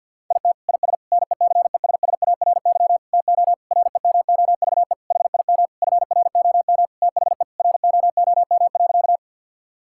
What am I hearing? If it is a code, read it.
Morse code, heard as a keyed tone, about 50 wpm, 700 Hz. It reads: IT IS NECESSARY TO REMOVE HIM FROM THE ROOM.
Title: Sentences from Top 1000 Words - Review Part 5 - 50wpm